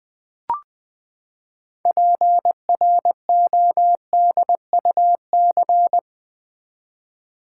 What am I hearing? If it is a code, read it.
Morse code, heard as a keyed tone, about 20 wpm, 700 Hz. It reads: PRODUC
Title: Build A Word - 6 Letters At A Time - 12-Letter Words - 20wpm